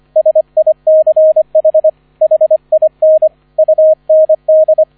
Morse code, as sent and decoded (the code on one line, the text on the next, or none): ... .. -.-. .... / .... .. -. / ..- -. -..
SICH HIN UND